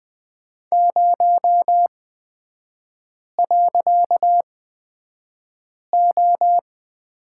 Morse code, heard as a keyed tone, about 20 wpm, spaced as for 5 wpm, 700 Hz. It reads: 0.O
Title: 255 Lesson 41e - Period - Three Characters 20wpm